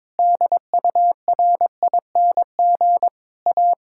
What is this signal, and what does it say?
Morse code, heard as a keyed tone, about 22 wpm, 700 Hz. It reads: DURING A
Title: Sentences from Top 1000 Words - Review Part 5 - 22wpm